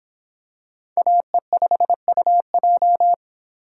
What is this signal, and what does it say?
Morse code, heard as a keyed tone, about 25 wpm, 700 Hz. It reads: AE5UJ